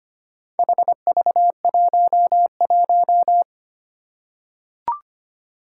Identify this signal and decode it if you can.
Morse code, heard as a keyed tone, about 25 wpm, 700 Hz. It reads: HV11
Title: Four Challenging Letter-Number Combinations 25wpm